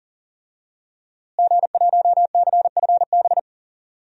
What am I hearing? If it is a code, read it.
Morse code, heard as a keyed tone, about 40 wpm, 700 Hz. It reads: G1CFB
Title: DXCC-100 Call Signs - 5 Characters - Repeated 2x - 40wpm